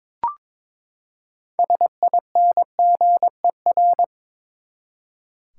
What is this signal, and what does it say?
Morse code, heard as a keyed tone, about 21 wpm, 700 Hz. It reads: SINGER
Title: Six Letter Words - 22wpm